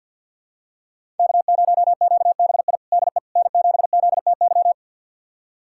Morse code, heard as a keyed tone, about 50 wpm, 700 Hz. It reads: K0QBI DE N7ZTY
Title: USA Callsign DE Callsign - 5 Characters - 50wpm